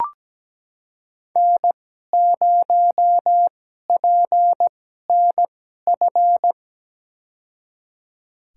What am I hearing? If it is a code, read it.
Morse code, heard as a keyed tone, about 17 wpm, spaced as for 12 wpm, 700 Hz. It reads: N0PNF
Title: USA Call Signs - 2x Character Spacing - 17wpm